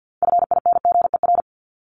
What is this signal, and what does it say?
Morse code, heard as a keyed tone, about 50 wpm, 700 Hz. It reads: FINGER
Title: Top 1501-1750 Words 50wpm